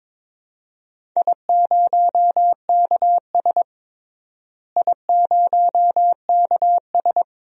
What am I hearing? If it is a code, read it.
Morse code, heard as a keyed tone, about 22 wpm, 700 Hz. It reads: I0KS I0KS
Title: DXCC-100 Call Signs - 4 Characters - Repeated 2x - 22wpm